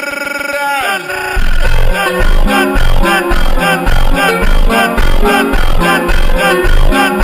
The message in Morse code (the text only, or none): none